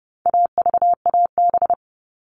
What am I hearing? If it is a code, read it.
Morse code, heard as a keyed tone, about 30 wpm, 700 Hz. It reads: AVAB
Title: All Cut Numbers - 4-Digit 30wpm